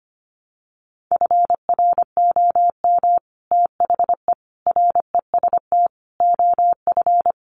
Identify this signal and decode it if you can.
Morse code, heard as a keyed tone, about 25 wpm, 700 Hz. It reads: FROM THE REST OF
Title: Sets of 5 Words 25wpm